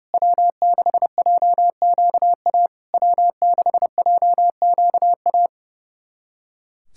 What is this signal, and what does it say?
Morse code, heard as a keyed tone, about 30 wpm, 700 Hz. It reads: W6JQA W6JQA
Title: SOTA QSO Practice 30wpm